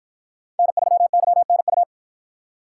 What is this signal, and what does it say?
Morse code, heard as a keyed tone, about 53 wpm, 700 Hz. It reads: N3YNU